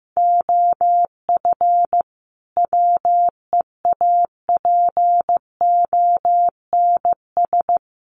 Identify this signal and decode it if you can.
Morse code, heard as a keyed tone, about 15 wpm, 700 Hz. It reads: OF WEAPONS
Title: Sets of 5 Words Encore 15wpm